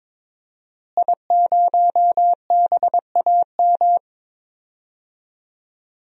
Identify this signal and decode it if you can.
Morse code, heard as a keyed tone, about 21 wpm, 700 Hz. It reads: I0BAM
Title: DXCC-100 Call Signs - 5 Characters - Repeated 3x - 22wpm